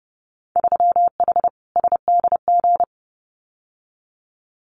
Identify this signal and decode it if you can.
Morse code, heard as a keyed tone, about 30 wpm, 700 Hz. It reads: 3H SDG
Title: Winter Field Day Exchanges Encore 30wpm